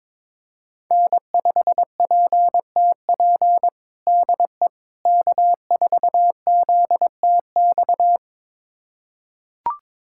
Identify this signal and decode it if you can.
Morse code, heard as a keyed tone, about 22 wpm, 700 Hz. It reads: N5PTP DE K4ZTX
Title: USA Callsign DE Callsign - 5 Characters - 22wpm